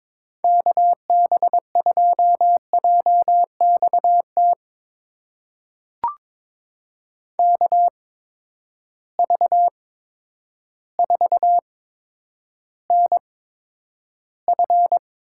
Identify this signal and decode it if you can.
Morse code, heard as a keyed tone, about 22 wpm, spaced as for 10 wpm, 700 Hz. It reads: KB2JXT KV4NF
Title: USA Call Signs - 8x Character Spacing - 22wpm